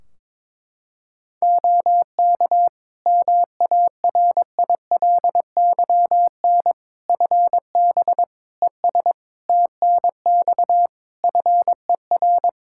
Morse code, heard as a keyed tone, about 22 wpm, 700 Hz. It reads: OK MARILYN FB ES TNX FER
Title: Practice QSO - Exchange 2 - Info - 22wpm